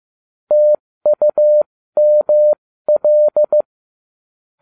Morse code, heard as a keyed tone, about 15 wpm, spaced as for 13 wpm, 600 Hz. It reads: TUML